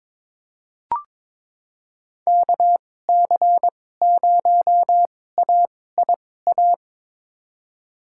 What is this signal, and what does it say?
Morse code, heard as a keyed tone, about 22 wpm, spaced as for 16 wpm, 700 Hz. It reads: KC0AIA
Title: USA Call Signs - 2x Character Spacing - 22wpm